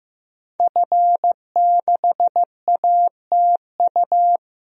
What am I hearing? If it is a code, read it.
Morse code, heard as a keyed tone, about 15 wpm, 700 Hz. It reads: F6ATU
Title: DXCC-100 Call Signs - 5 Characters - Repeated 3x - 15wpm